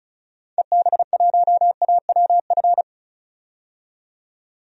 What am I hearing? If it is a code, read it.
Morse code, heard as a keyed tone, about 36 wpm, 700 Hz. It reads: EB1AWF
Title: DXCC-100 Call Signs - 6 Characters - 35wpm